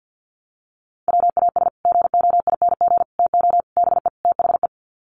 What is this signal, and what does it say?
Morse code, heard as a keyed tone, about 50 wpm, 700 Hz. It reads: WAS GOING TO BE THE
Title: Sets of 5 Words Encore2 50wpm